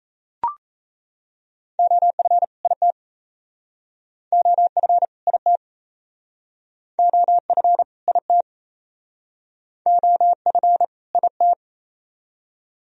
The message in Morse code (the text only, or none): OF IT OF IT OF IT OF IT